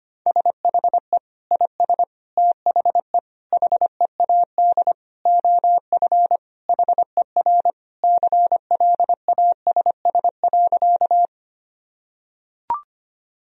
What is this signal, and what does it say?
Morse code, heard as a keyed tone, about 25 wpm, 700 Hz. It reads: SHE IS THE HEAD OF HER CLASS.
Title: Sentences from Top 1000 Words - Review Part 2 - 25wpm